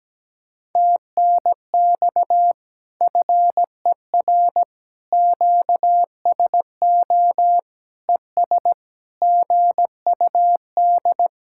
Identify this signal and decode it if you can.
Morse code, heard as a keyed tone, about 17 wpm, 700 Hz. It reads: TNX FER QSO ES GUD
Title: Speed-Racing Common QSO Phrases Encore - 17wpm